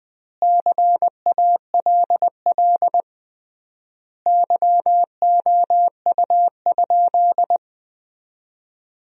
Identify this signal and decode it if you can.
Morse code, heard as a keyed tone, about 20 wpm, 700 Hz. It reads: CALL YOU?
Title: 3x Word-Spacing - Sentences from Top 100 Words - 20wpm